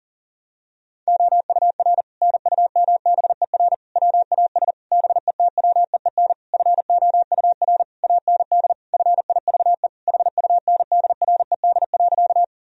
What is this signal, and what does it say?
Morse code, heard as a keyed tone, about 40 wpm, 700 Hz. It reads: OUR NUMBER WAS BETWEEN FOUR AND FIVE HUNDRED.
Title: Sentences from Top 1000 Words - Review Part 2 - 40wpm